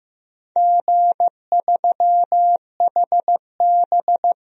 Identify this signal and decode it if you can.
Morse code, heard as a keyed tone, about 15 wpm, 700 Hz. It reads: G3HB
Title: DXCC-100 Call Signs - 4 Characters - Repeated 3x - 15wpm